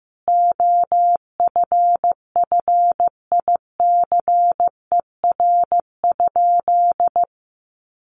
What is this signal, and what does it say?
Morse code, heard as a keyed tone, about 15 wpm, 700 Hz. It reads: OFFICER?
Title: Sentences from Top 1200 Words 15wpm